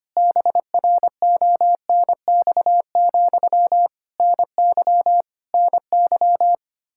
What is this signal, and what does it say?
Morse code, heard as a keyed tone, about 25 wpm, 700 Hz. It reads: BRONX, NY NY
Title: Practice QSO - Exchange 1 - Beginning - 25wpm